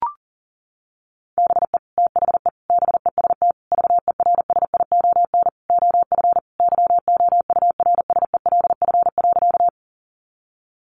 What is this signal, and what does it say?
Morse code, heard as a keyed tone, about 40 wpm, 700 Hz. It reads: BE THE BEST VERSION OF YOURSELF.
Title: Sentences from Top 1400 Words 40wpm